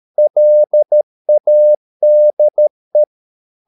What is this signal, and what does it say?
Morse code, heard as a keyed tone, about 13 wpm, 600 Hz. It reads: LADE
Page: Ferie 13 wpm